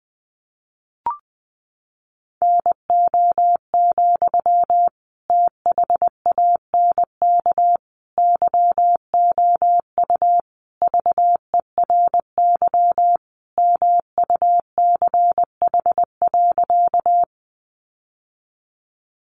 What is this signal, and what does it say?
Morse code, heard as a keyed tone, about 20 wpm, 700 Hz. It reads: NO, THANK YOU VERY MUCH.
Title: Sentences from Top 600 Words 20wpm